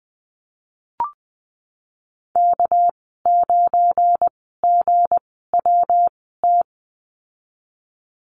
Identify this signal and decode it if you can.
Morse code, heard as a keyed tone, about 20 wpm, 700 Hz. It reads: K 9 G W T
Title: 238 Lesson 38g - Letter q - Call Signs 20wpm